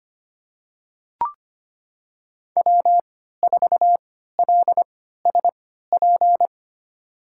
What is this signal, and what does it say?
Morse code, heard as a keyed tone, about 25 wpm, spaced as for 14 wpm, 700 Hz. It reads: W4LSP
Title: USA Call Signs - 5 Characters - 3x Character Spacing - 25wpm